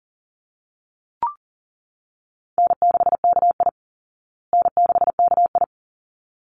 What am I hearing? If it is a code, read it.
Morse code, heard as a keyed tone, about 40 wpm, 700 Hz. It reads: N6KI N6KI
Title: Call Signs Repeated 3x - 40wpm